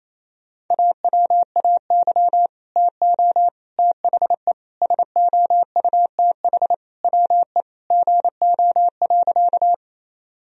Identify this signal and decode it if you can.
Morse code, heard as a keyed tone, about 28 wpm, 700 Hz. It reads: AWAY TO THE SOUTH WE GO.
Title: Sentences from Top 1000 Words - Review Part 2 - 28wpm